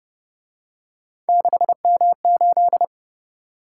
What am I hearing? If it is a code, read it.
Morse code, heard as a keyed tone, about 29 wpm, 700 Hz. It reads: 6M8